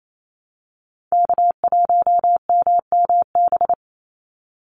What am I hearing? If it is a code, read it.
Morse code, heard as a keyed tone, about 28 wpm, 700 Hz. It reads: K1MMB
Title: Call Signs - 28wpm